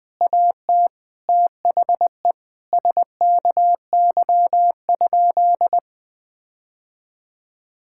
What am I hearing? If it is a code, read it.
Morse code, heard as a keyed tone, about 20 wpm, 700 Hz. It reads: AT THE SKY?
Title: Sentences with 4-Word Phrases - Spoken Beginning - 20wpm